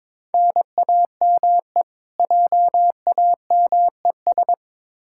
Morse code, heard as a keyed tone, about 22 wpm, 700 Hz. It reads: NAME JAMES
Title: Practice QSO - Exchange 1 - Beginning - 22wpm